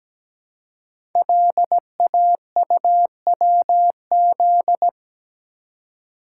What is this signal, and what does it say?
Morse code, heard as a keyed tone, about 17 wpm, 700 Hz. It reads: LAUWZ
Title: Code Groups - 5 Letter:Numbers - 17wpm